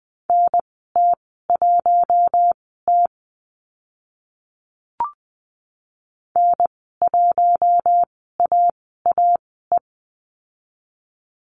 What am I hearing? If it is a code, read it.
Morse code, heard as a keyed tone, about 20 wpm, spaced as for 14 wpm, 700 Hz. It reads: NT1T N1AAE
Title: 048 Lesson 9g - Number 4 - Call Signs 20wpm